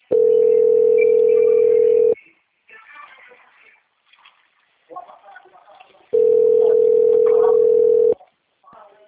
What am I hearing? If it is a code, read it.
Ringback tone (North America), 440 + 480 Hz, with speech.
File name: out- - - - - .81870